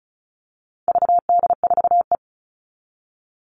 Morse code, heard as a keyed tone, about 34 wpm, 700 Hz. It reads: VD4E
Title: All Cut Numbers - 4-Digit 35wpm